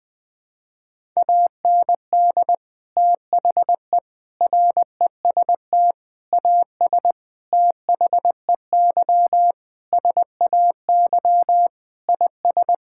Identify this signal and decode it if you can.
Morse code, heard as a keyed tone, about 21 wpm, 700 Hz. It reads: AND THE REST AS THEY SAY IS
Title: Sentences with 8-Word Phrases - Spoken Beginning - 20wpm